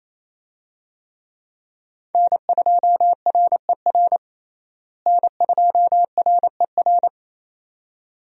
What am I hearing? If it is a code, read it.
Morse code, heard as a keyed tone, about 28 wpm, 700 Hz. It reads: N2RER N2RER
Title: USA Call Signs - 5 Characters - Repeated 2x - 28wpm